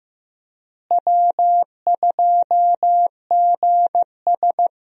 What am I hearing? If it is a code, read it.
Morse code, heard as a keyed tone, about 15 wpm, 700 Hz. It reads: W2GS
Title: USA Call Signs - 4 Characters - Repeated 2x - 15wpm